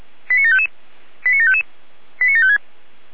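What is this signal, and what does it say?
Five-tone selective call (ZVEI), heard as quick runs of five tones, tones of about 70 ms. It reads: 89755 89755 89756
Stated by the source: Five-tone ZVEI selective call received on SDR.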